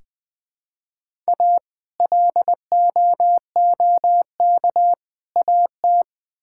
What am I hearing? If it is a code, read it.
Morse code, heard as a keyed tone, about 20 wpm, 700 Hz. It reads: A LOOK AT
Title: Build A Phrase - 3-Word Phrases - 20wpm